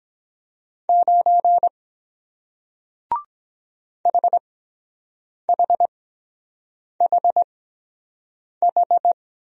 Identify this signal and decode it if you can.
Morse code, heard as a keyed tone, about 23 wpm, spaced as for 7 wpm, 700 Hz. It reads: 9 HHHH